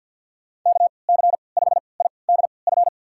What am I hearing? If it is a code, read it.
Morse code, heard as a keyed tone, about 48 wpm, spaced as for 28 wpm, 690 Hz. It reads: KC5IDF
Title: USA Call Signs - 6 Characters - 3x Character Spacing - 50wpm